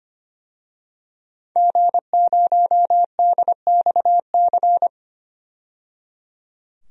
Morse code, heard as a keyed tone, about 25 wpm, 700 Hz. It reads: G0DXC
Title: DXCC-100 Call Signs - 5 Characters - 25wpm